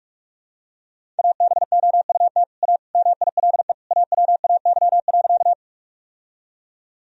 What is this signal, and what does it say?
Morse code, heard as a keyed tone, about 42 wpm, 700 Hz. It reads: ABOUT A MILE AWAY.
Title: Sentences from Top 1000 Words - Review Part 1 - 45wpm